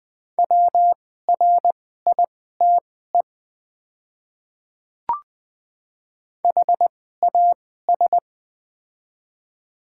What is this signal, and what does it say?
Morse code, heard as a keyed tone, about 20 wpm, spaced as for 14 wpm, 700 Hz. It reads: WRITE HAS
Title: 2x Character Spacing - Top 300 Words - 20wpm